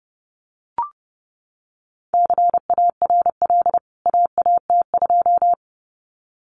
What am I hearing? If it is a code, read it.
Morse code, heard as a keyed tone, about 30 wpm, 700 Hz. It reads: CARL AAT2
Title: CWT Exchange 30wpm